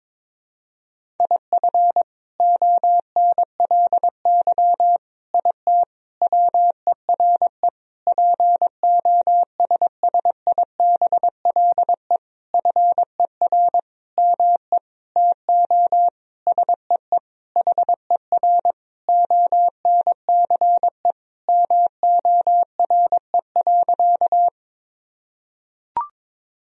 Morse code, heard as a keyed tone, about 22 wpm, 700 Hz. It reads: IF ONLY IT WERE POSSIBLE FER ME TO SEE HER ONCE MORE.
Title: Sentences from Top 500 Words and Abbreviations 22wpm